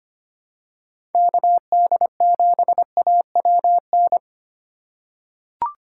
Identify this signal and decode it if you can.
Morse code, heard as a keyed tone, about 25 wpm, 700 Hz. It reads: KD7AWN
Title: USA Call Signs - 6 Characters - 4x Character Spacing - 25wpm